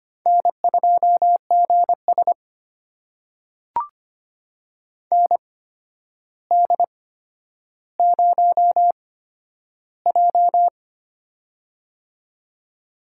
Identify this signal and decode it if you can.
Morse code, heard as a keyed tone, about 25 wpm, spaced as for 11 wpm, 700 Hz. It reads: N2GS ND0J
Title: USA Call Signs - 4 Characters - 8x Character Spacing - 25wpm